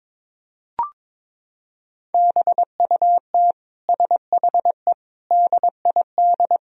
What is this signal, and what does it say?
Morse code, heard as a keyed tone, about 22 wpm, 700 Hz. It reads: BUT SHE DID
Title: Sentences from Top 1000 Words - Review Part 1 - 22wpm